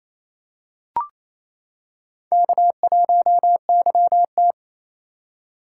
Morse code, heard as a keyed tone, about 27 wpm, 700 Hz. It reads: K1YT
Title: K1USN SST QSO Practice - 28wpm